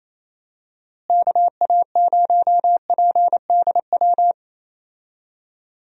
Morse code, heard as a keyed tone, about 27 wpm, 700 Hz. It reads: KA0PDW
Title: USA Call Signs - 6 Characters - Repeated 3x - 28wpm